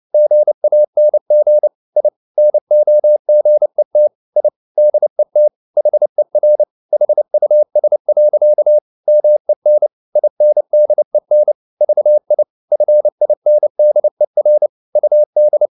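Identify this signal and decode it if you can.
Morse code, heard as a keyed tone, about 29 wpm, 600 Hz. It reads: GANG I NOGET I DET HER HUS. MEN INDEN VI FINDER UD
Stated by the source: Fast (QRQ) Morse at 29 wpm, from CW med Gnister.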